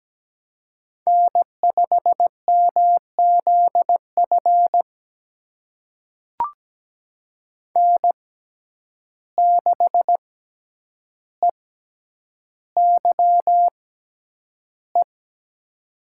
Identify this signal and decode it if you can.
Morse code, heard as a keyed tone, about 17 wpm, spaced as for 9 wpm, 700 Hz. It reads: N5MZF N6EYE
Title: USA Call Signs - 5 Characters - 6x Character Spacing - 17wpm